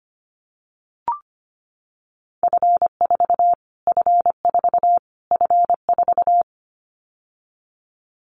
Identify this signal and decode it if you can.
Morse code, heard as a keyed tone, about 25 wpm, 700 Hz. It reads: F4 F4 F4
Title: Callsign Prefixes for Top 25 DXCC - Repeated 3x - 25wpm